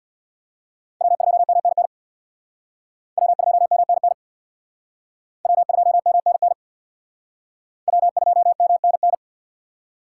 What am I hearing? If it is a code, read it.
Morse code, heard as a keyed tone, about 62 wpm, 700 Hz. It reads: W2MNN W2MNN W2MNN W2MNN
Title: Speed Racing - USA Call Signs 50wpm